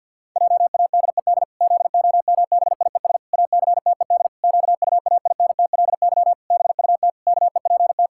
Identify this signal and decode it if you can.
Morse code, heard as a keyed tone, about 52 wpm, 700 Hz. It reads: JADED ZOMBIES ACTED QUAINTLY BUT KEPT
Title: Pangram Challenge v2 - 50wpm